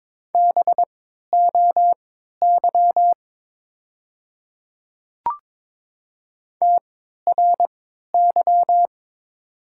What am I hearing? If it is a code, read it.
Morse code, heard as a keyed tone, about 22 wpm, spaced as for 13 wpm, 700 Hz. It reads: BOY TRY
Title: 3x Character Spacing - Top 300 Words - 22wpm